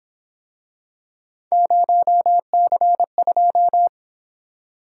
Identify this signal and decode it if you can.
Morse code, heard as a keyed tone, about 28 wpm, 700 Hz. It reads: 0C2